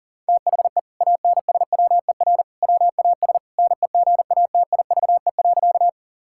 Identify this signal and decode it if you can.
Morse code, heard as a keyed tone, about 40 wpm, 700 Hz. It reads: THE ANSWER WAS NEGATIVE.
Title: Sentences from Top 1000 Words - Review Part 6 - 40wpm